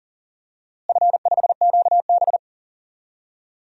Morse code, heard as a keyed tone, about 40 wpm, 700 Hz. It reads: F5QB
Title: DXCC-100 Call Signs - 4 Characters - Repeated 3x - 40wpm